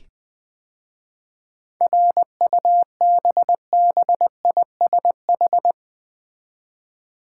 Morse code, heard as a keyed tone, about 20 wpm, 700 Hz. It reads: RUBBISH
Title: 7 Letter Words - First 3 Letters Given - 20wpm